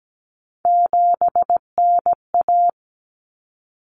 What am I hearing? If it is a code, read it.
Morse code, heard as a keyed tone, about 17 wpm, 700 Hz. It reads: 7NA
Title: Common Cut Numbers - 3-Digit 17wpm